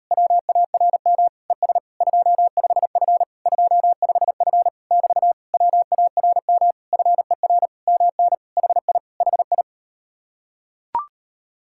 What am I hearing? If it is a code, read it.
Morse code, heard as a keyed tone, about 38 wpm, 700 Hz. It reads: WARM ES 25F 25F <BT> WARM FER MN HI HI